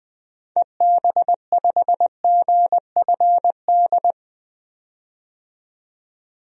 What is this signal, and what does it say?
Morse code, heard as a keyed tone, about 20 wpm, 700 Hz. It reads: EB5GFD
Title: DXCC-100 Call Signs - 6 Characters - 20wpm